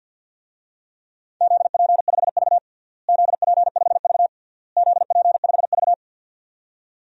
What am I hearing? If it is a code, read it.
Morse code, heard as a keyed tone, about 53 wpm, 700 Hz. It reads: ZP5V ZP5V ZP5V
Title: DXCC-100 Call Signs - 4 Characters - Repeated 3x - 50wpm